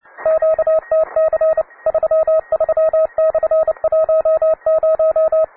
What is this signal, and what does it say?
Morse code, heard as a keyed tone, about 30 wpm, 640 Hz, with radio noise. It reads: QTC 33/10
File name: wae / cw beginner / ra9ac cw